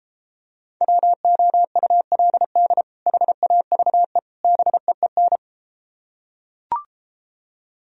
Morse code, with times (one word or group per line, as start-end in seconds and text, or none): 0.8-2.8 WOULD
3.1-4.2 HAVE
4.4-5.4 BEEN